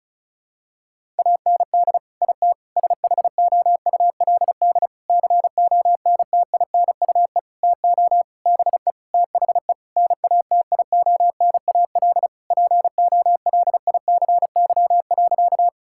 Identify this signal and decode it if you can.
Morse code, heard as a keyed tone, about 34 wpm, 700 Hz. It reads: AND IT SHOULD CONTINUE TO BE THE NATIONAL POLICY.
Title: Sentences from Top 1000 Words - Review Part 6 - 35wpm